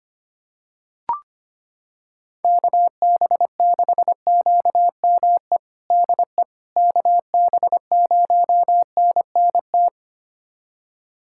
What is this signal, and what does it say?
Morse code, heard as a keyed tone, about 25 wpm, 700 Hz. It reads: KB6QME DE KB0NNT
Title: USA Callsign DE Callsign - 6 Characters - 25wpm